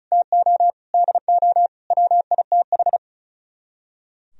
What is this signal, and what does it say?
Morse code, heard as a keyed tone, about 42 wpm, 700 Hz. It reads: TO DO WITH
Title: Speed-Racing - Sets of 5 Words 35wpm